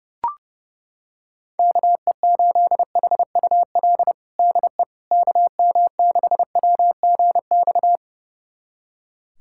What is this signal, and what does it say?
Morse code, heard as a keyed tone, about 30 wpm, 700 Hz. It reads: KE8HUL DE KM6WGX
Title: USA Callsign DE Callsign 30wpm